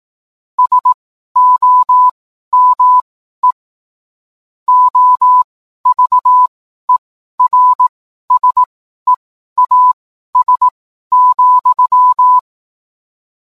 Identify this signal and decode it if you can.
Morse code, heard as a keyed tone, about 18 wpm, spaced as for 12 wpm, 1000 Hz. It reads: SOME OVERSEAS,
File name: cwblog / cw mp3 directory / mem morse 12